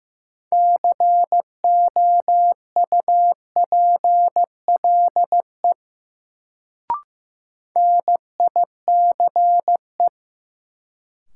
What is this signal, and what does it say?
Morse code, heard as a keyed tone, about 15 wpm, 700 Hz. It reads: COUPLE NICE
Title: Sentences from Top 1100 Words - New Word Drill - 15wpm